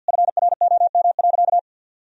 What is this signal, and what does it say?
Morse code, heard as a keyed tone, about 50 wpm, 700 Hz. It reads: FROM.
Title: Sentences from Top 1000 Words - Review Part 4 - 50wpm